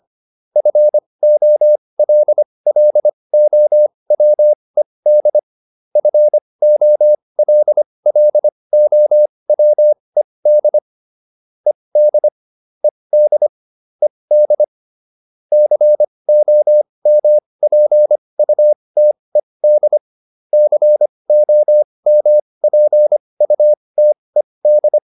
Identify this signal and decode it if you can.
Morse code, heard as a keyed tone, about 25 wpm, spaced as for 20 wpm, 600 Hz. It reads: FOLLOWED FOLLOWED ED ED ED COMPUTED COMPUTED
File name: ED8 20